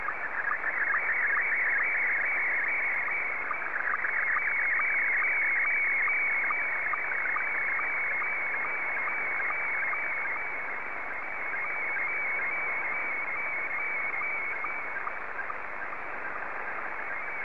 SSTV (slow-scan television), Scottie 1.